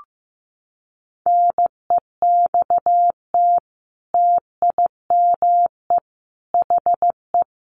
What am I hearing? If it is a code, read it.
Morse code, heard as a keyed tone, about 15 wpm, 700 Hz. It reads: NEXT TIME HE
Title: A Sound Of Thunder 15wpm